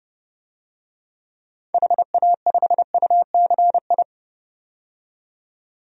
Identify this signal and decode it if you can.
Morse code, heard as a keyed tone, about 32 wpm, 700 Hz. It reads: HA5UCI
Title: DXCC-100 Call Signs - 6 Characters - 30wpm